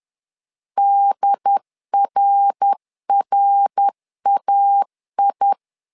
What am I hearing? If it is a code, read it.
Morse code, heard as a keyed tone, about 10 wpm, 800 Hz. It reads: DRRAI
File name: LEZIONI / 3 MWRO / 3 MWR 10wpm 1952